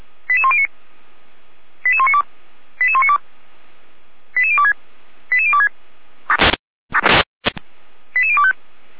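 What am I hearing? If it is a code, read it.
Five-tone selective call (ZVEI), heard as quick runs of five tones, tones of about 70 ms. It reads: 88109 88192 88192 88026 88026 88025
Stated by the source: ZVEI five-tone call received by SDR, from audio.f85360.feuerwehr-orlamuende.org (Feuerwehr Orlamünde fire brigade).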